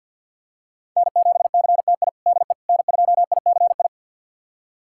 Morse code, heard as a keyed tone, about 52 wpm, 700 Hz. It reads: N7CTI DE N2ICI
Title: USA Callsign DE Callsign - 5 Characters - 50wpm